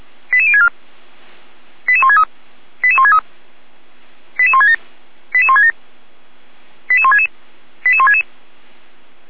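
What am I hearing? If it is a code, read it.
Five-tone selective call (ZVEI), heard as quick runs of five tones, tones of about 80 ms. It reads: 88064 88163 88163 88167 88167 88166 88166
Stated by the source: Five-tone ZVEI selective call received on an SDR.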